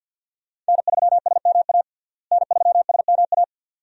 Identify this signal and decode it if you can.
Morse code, heard as a keyed tone, about 49 wpm, 700 Hz. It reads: N3SMA N3SMA
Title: USA Call Signs - 5 Characters - Repeated 2x - 50wpm